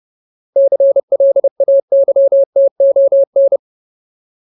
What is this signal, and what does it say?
Morse code, heard as a keyed tone, about 30 wpm, 550 Hz. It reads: CLAYTON
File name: Ultimate Word List / 30WPM Ultimate